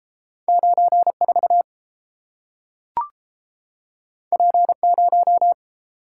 Morse code, heard as a keyed tone, about 30 wpm, 700 Hz. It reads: 94 P0